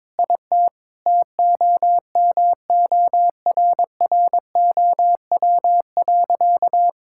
Morse code, heard as a keyed tone, about 22 wpm, 700 Hz. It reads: IT TOMORROW.
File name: practice / 22wpm / Sentences with 2-Word Phrases - 22wpm